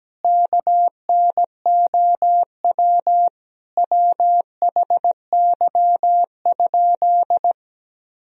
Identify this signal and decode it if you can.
Morse code, heard as a keyed tone, about 17 wpm, 700 Hz. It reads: KNOW WHY?
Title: Sentences from Top 200 Words and Abbreviations 17wpm